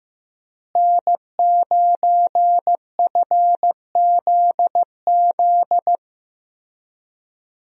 Morse code, heard as a keyed tone, about 15 wpm, 700 Hz. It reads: N9FZZ
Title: USA Call Signs - 5 Characters - Repeated 3x - 15wpm